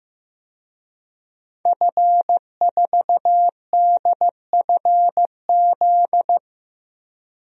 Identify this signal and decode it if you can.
Morse code, heard as a keyed tone, about 15 wpm, 700 Hz. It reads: F4DFZ
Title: DXCC-100 Call Signs - 5 Characters - Repeated 2x - 15wpm